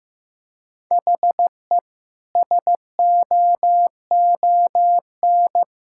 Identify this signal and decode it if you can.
Morse code, heard as a keyed tone, about 15 wpm, 700 Hz. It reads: HE SOON
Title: Sentences from Top 1600 Words 15wpm